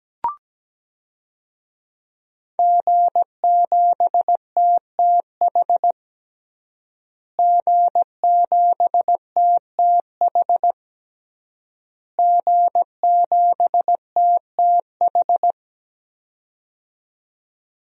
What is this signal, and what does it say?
Morse code, heard as a keyed tone, about 17 wpm, 700 Hz. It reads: G7TTH G7TTH G7TTH
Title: DXCC-100 Call Signs - 5 Characters - Repeated 3x - 17wpm